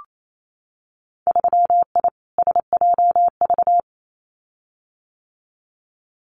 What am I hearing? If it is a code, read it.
Morse code, heard as a keyed tone, about 28 wpm, 700 Hz. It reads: 3I SJV
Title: Winter Field Day Exchanges Encore 28wpm